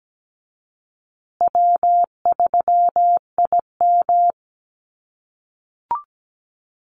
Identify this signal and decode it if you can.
Morse code, heard as a keyed tone, about 17 wpm, 700 Hz. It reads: W3IM
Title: USA Call Signs Repeated 2x 17wpm